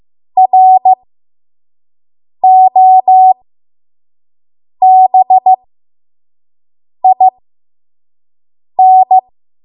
Morse code, heard as a keyed tone, about 15 wpm, spaced as for 5 wpm, 750 Hz. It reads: ROBIN
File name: Morse / 05WPM